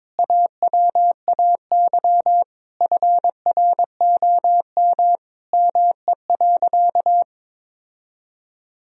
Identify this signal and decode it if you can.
Morse code, heard as a keyed tone, about 22 wpm, 700 Hz. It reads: AWAY FROM ME.
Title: Sentences from Top 1000 Words - Review Part 1 - 22wpm